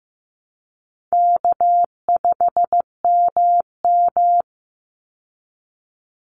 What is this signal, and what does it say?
Morse code, heard as a keyed tone, about 15 wpm, 700 Hz. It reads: K5MM
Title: Easy Call Signs Repeated 3x - 15wpm